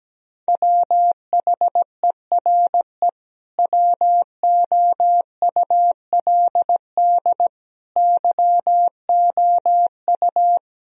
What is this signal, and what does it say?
Morse code, heard as a keyed tone, about 17 wpm, 700 Hz. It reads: WHERE WOULD YOU
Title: Sentences from Top 1000 Words - Review Part 3 - 17wpm